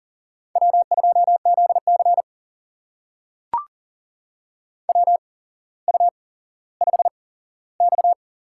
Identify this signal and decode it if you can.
Morse code, heard as a keyed tone, about 38 wpm, spaced as for 17 wpm, 700 Hz. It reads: W2ZC WU5X
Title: USA Call Signs - 4 Characters - 8x Character Spacing - 40wpm